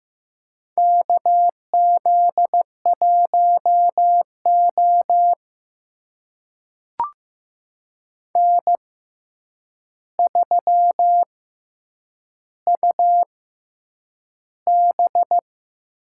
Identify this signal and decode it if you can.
Morse code, heard as a keyed tone, about 15 wpm, spaced as for 8 wpm, 700 Hz. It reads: KZ1O N3UB
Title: USA Call Signs - 4 Characters - 6x Character Spacing - 15wpm